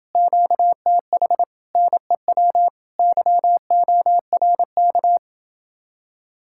Morse code, heard as a keyed tone, about 30 wpm, 700 Hz. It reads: QTH NEW YORK